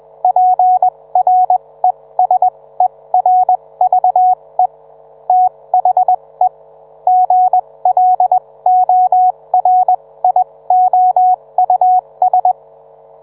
Morse code, heard as a keyed tone, about 21 wpm, spaced as for 17 wpm, 740 Hz. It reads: PRESERVE THE GLORIOUS